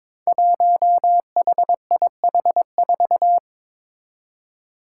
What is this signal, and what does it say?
Morse code, heard as a keyed tone, about 21 wpm, 700 Hz. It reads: 1HIH4
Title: Five Challenging Letter-Number Combinations 22wpm